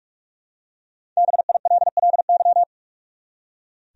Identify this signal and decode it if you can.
Morse code, heard as a keyed tone, about 46 wpm, 700 Hz. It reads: BILLY
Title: USA Common Names - 45wpm